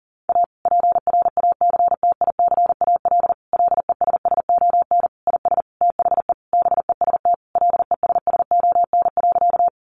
Morse code, heard as a keyed tone, about 40 wpm, 700 Hz. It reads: A PRACTICAL LESSON IS THE BEST LESSON.
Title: Sentences from Top 1400 Words 40wpm